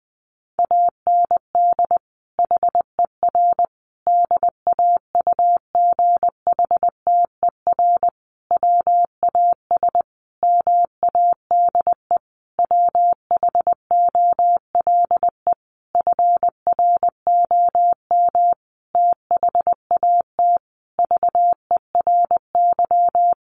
Morse code, heard as a keyed tone, about 20 wpm, 700 Hz. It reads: AND HER DAUGHTER WAS MADE WHOLE FROM THAT VERY
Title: Sentences from Top 1500 Words 20wpm